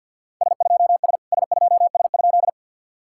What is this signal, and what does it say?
Morse code, heard as a keyed tone, about 46 wpm, 660 Hz. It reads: S2S S2S?
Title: SOTA QSO Practice 50wpm